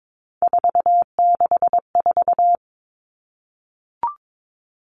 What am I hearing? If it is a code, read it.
Morse code, heard as a keyed tone, about 22 wpm, 700 Hz. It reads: 464
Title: Three Challenging Letter-Number Combinations 22wpm